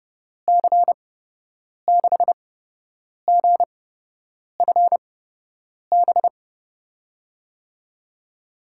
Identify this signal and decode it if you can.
Morse code, heard as a keyed tone, about 31 wpm, spaced as for 8 wpm, 700 Hz. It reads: C6GFB
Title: USA Call Signs - 6 Characters - 8x Character Spacing - 30wpm